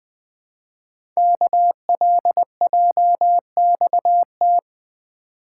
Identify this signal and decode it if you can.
Morse code, heard as a keyed tone, about 20 wpm, 700 Hz. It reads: KLJXT
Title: Code Groups - 5 Letter:Numbers - 20wpm